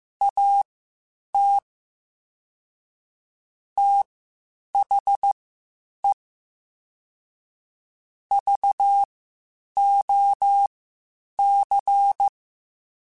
Morse code, heard as a keyed tone, about 15 wpm, spaced as for 8 wpm, 800 Hz. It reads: AT THE VOC
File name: QSOs / QSOs-mp3s / WD0BGP 1@10